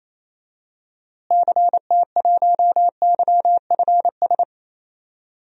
Morse code, heard as a keyed tone, about 28 wpm, 700 Hz. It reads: CT1YFS
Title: DXCC-100 Call Signs - 6 Characters - Repeated 3x - 28wpm